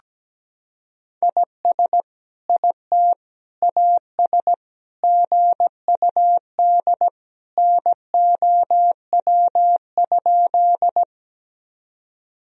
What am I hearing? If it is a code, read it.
Morse code, heard as a keyed tone, about 17 wpm, 700 Hz. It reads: IS IT AS GUD NOW?
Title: Sentences from Top 200 Words and Abbreviations 17wpm